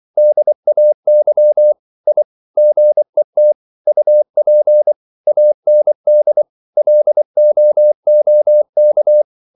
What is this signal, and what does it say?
Morse code, heard as a keyed tone, about 24 wpm, 600 Hz. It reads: DAY I GET UP AND LOOK
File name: CW / Practice / EnglishQuotations0324